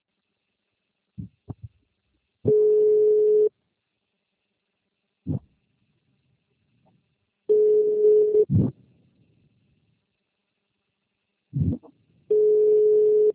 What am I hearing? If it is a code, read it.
Ringback tone (Europe, 425 Hz).